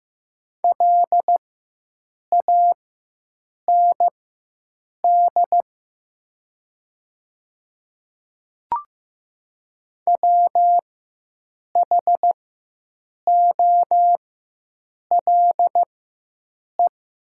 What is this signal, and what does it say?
Morse code, heard as a keyed tone, about 15 wpm, spaced as for 7 wpm, 700 Hz. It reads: LAND WHOLE
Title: 4x Character Spacing - Top 500 Words - 15wpm